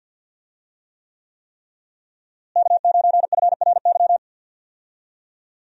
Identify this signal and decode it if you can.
Morse code, heard as a keyed tone, about 50 wpm, 700 Hz. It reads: K9FRY
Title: USA Call Signs - 5 Characters - Repeated 3x - 50wpm